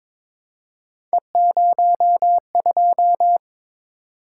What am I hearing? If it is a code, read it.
Morse code, heard as a keyed tone, about 21 wpm, 700 Hz. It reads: E02
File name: practice / 22wpm / Three Letter-Number Combinations - 22wpm